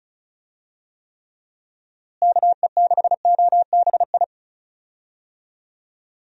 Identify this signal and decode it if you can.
Morse code, heard as a keyed tone, about 34 wpm, 700 Hz. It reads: KE6OBI
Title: USA Call Signs - 6 Characters - Repeated 3x - 35wpm